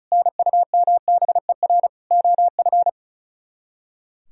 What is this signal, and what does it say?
Morse code, heard as a keyed tone, about 45 wpm, 700 Hz. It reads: NUMBER OF